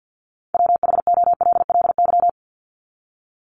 Morse code, heard as a keyed tone, about 50 wpm, 700 Hz. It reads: WHOLLY
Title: LY Suffix - Top 100 Words with Base Words 50wpm